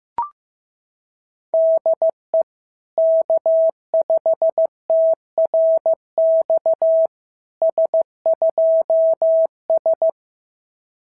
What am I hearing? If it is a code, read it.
Morse code, heard as a keyed tone, about 15 wpm, 650 Hz. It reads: DE K5TRX S2S
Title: SOTA QSO Practice 15wpm